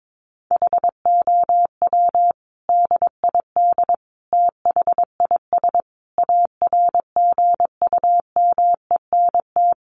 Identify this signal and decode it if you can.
Morse code, heard as a keyed tone, about 22 wpm, 700 Hz. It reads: HOW DID THIS ARGUMENT
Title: Sentences from Top 1400 Words 22wpm